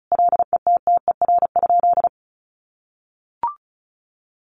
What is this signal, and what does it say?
Morse code, heard as a keyed tone, about 34 wpm, 700 Hz. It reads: LETTER?
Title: Sentences from Top 300 Words 35wpm